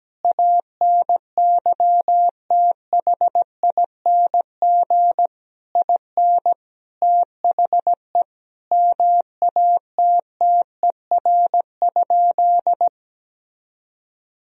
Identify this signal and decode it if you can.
Morse code, heard as a keyed tone, about 17 wpm, 700 Hz. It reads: ANYTHING IN THE MATTER?
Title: Sentences from Top 1000 Words - Review Part 4 - 17wpm